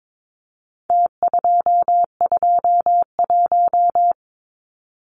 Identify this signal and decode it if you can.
Morse code, heard as a keyed tone, about 22 wpm, 700 Hz. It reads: T221
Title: Numbers - Contest Style - 4-Digit 22wpm